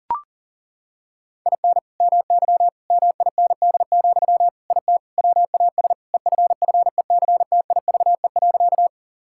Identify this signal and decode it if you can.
Morse code, heard as a keyed tone, about 37 wpm, 700 Hz. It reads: IN MY MIND, IT WAS EFFECTIVE.
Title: Sentences from Top 1000 Words - Review Part 6 - 40wpm